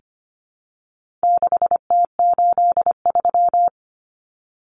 Morse code, heard as a keyed tone, about 25 wpm, 700 Hz. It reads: 6T83
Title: Common Cut Numbers - 4-Digit 25wpm